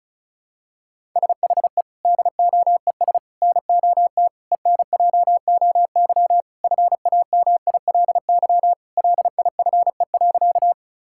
Morse code, heard as a keyed tone, about 36 wpm, 700 Hz. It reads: SHE DOES NOT ENJOY FAMILY LIFE.
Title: Sentences from Top 1000 Words - Review Part 5 - 35wpm